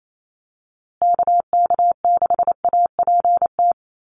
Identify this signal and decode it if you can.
Morse code, heard as a keyed tone, about 28 wpm, 700 Hz. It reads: KK6APT
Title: USA Call Signs Repeated 2x 28wpm